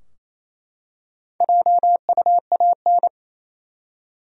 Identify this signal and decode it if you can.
Morse code, heard as a keyed tone, about 28 wpm, 700 Hz. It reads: JUAN
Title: USA Common Names - 28wpm